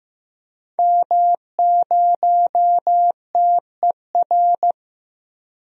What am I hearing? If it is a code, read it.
Morse code, heard as a keyed tone, about 15 wpm, 700 Hz. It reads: M0TER
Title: DXCC-100 Call Signs - 5 Characters - Repeated 2x - 15wpm